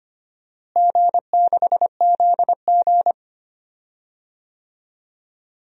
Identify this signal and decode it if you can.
Morse code, heard as a keyed tone, about 25 wpm, 700 Hz. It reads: G6ZG
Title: DXCC-100 Call Signs - 4 Characters - Repeated 3x - 25wpm